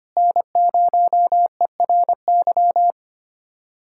Morse code, heard as a keyed tone, about 25 wpm, 700 Hz. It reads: N0ERY
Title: USA Callsign DE Callsign - 5 Characters - 25wpm